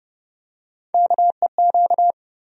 Morse code, heard as a keyed tone, about 30 wpm, 700 Hz. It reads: KEQ
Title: USA Call Sign Postfixes - 3 Letters 30wpm